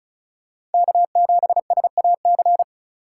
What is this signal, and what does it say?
Morse code, heard as a keyed tone, about 30 wpm, 700 Hz. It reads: K7SAC
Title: Speed Racing - USA 1x3 Call Signs 30wpm